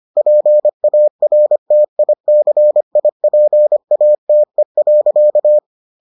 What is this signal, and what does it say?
Morse code, heard as a keyed tone, about 25 wpm, 600 Hz. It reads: PARTICIPATE.